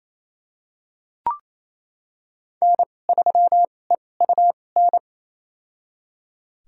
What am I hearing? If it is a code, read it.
Morse code, heard as a keyed tone, about 28 wpm, spaced as for 20 wpm, 700 Hz. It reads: N3EUN
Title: USA Call Signs - 5 Characters - 2x Character Spacing - 28wpm